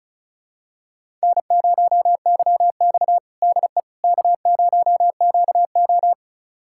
Morse code, heard as a keyed tone, about 36 wpm, 700 Hz. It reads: N0YX DE K0QO
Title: USA Callsign DE Callsign - 4 Characters - 35wpm